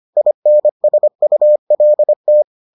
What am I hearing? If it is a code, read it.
Morse code, heard as a keyed tone, about 25 wpm, 600 Hz. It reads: INSULT